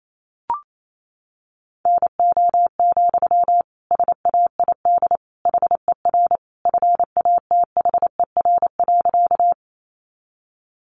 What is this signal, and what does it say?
Morse code, heard as a keyed tone, about 28 wpm, 700 Hz. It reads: NO, SAID HER FATHER.
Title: Sentences from Top 300 Words 28wpm